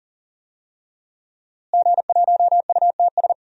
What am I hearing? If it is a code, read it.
Morse code, heard as a keyed tone, about 40 wpm, 700 Hz. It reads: G1UTS
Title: DXCC-100 Call Signs - 5 Characters - Repeated 2x - 40wpm